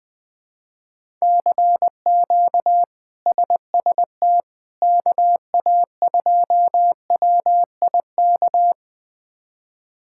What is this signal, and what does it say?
Morse code, heard as a keyed tone, about 20 wpm, 700 Hz. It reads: CQ SST KA2WIK
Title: K1USN SST QSO Practice - 20wpm